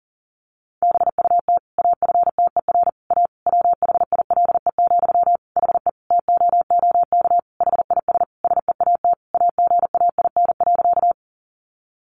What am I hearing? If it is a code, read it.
Morse code, heard as a keyed tone, about 40 wpm, 700 Hz. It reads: BUT AFTER A WHILE, HE TOOK HIS SEAT AGAIN.
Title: Sentences from Top 1500 Words 40wpm